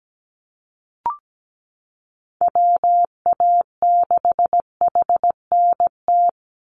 Morse code, heard as a keyed tone, about 17 wpm, 700 Hz. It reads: WA6HNT
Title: USA Call Signs 17wpm